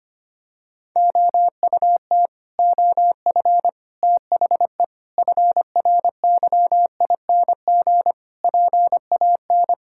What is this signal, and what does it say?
Morse code, heard as a keyed tone, about 25 wpm, 700 Hz. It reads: OUT OF THE FRYING PAN
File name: Sets of 9 Words - 25wpm